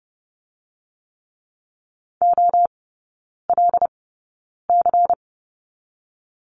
Morse code, heard as a keyed tone, about 32 wpm, spaced as for 9 wpm, 700 Hz. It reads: OLC
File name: Lesson 18e - Letter m - Three Characters 30wpm